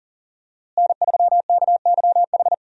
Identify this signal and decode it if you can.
Morse code, heard as a keyed tone, about 40 wpm, 700 Hz. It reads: N3KYH
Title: USA Call Signs - 5 Characters - Repeated 2x - 40wpm